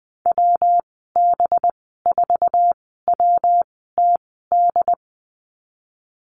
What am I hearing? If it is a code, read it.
Morse code, heard as a keyed tone, about 20 wpm, spaced as for 15 wpm, 700 Hz. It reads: WB4WTD
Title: 193 Lesson 31g - Letter v - Call Signs 20wpm